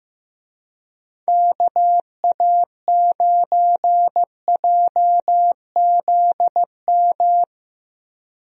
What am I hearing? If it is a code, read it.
Morse code, heard as a keyed tone, about 15 wpm, 700 Hz. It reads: KA9JZM
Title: USA Call Signs - 6 Characters - Repeated 2x - 15wpm